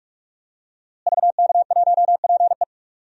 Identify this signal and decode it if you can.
Morse code, heard as a keyed tone, about 45 wpm, 700 Hz. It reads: VK1PE
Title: DXCC-100 Call Signs - 5 Characters - Repeated 3x - 45wpm